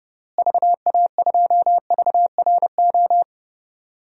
Morse code, heard as a keyed tone, about 30 wpm, 700 Hz. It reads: VA2VRO
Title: DXCC-100 Call Signs - 6 Characters - Repeated 2x - 30wpm